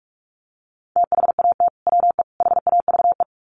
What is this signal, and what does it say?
Morse code, heard as a keyed tone, about 44 wpm, 700 Hz. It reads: THAT WE HAVE
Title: Sets of 3 Words 45wpm